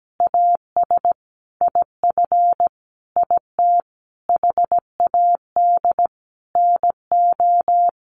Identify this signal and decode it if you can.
Morse code, heard as a keyed tone, about 17 wpm, 700 Hz. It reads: AS IF IT HAD NO
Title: Sentences from Top 1200 Words 17wpm